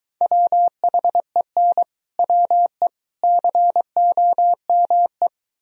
Morse code, heard as a keyed tone, about 25 wpm, 700 Hz. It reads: WHEN WE COME